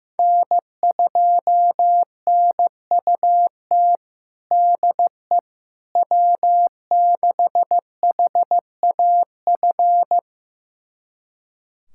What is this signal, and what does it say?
Morse code, heard as a keyed tone, about 15 wpm, 700 Hz. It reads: N2NUT DE W6HAF
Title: USA Callsign DE Callsign - 5 Characters - 15wpm